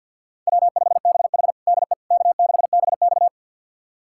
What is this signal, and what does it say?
Morse code, heard as a keyed tone, about 46 wpm, 700 Hz. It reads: W5BH DE K6BX
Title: USA Callsign DE Callsign - 4 Characters - 50wpm